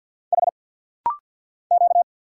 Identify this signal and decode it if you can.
Morse code, heard as a keyed tone, about 50 wpm, 700 Hz. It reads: H Q
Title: Single Letter-Number Alt - Rapid-Fire 50wpm